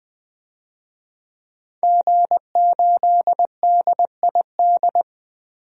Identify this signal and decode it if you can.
Morse code, heard as a keyed tone, about 20 wpm, 700 Hz. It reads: G8DID
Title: DXCC-100 Call Signs - 5 Characters - Repeated 3x - 20wpm